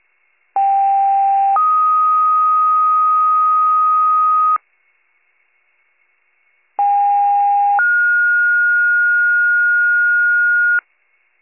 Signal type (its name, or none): two-tone page